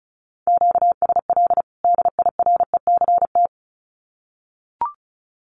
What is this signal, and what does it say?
Morse code, heard as a keyed tone, about 34 wpm, 700 Hz. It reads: QSL DIRECT
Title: Common QSO Phrases 35wpm